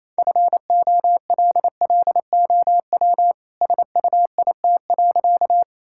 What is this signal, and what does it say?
Morse code, heard as a keyed tone, about 28 wpm, 700 Hz. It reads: FOLLOW SUIT.
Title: Sentences from Top 1600 Words 28wpm